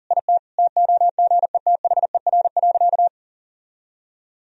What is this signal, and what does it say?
Morse code, heard as a keyed tone, about 40 wpm, 700 Hz. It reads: IT TOGETHER.
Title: Sentences from Top 1000 Words - Review Part 1 - 40wpm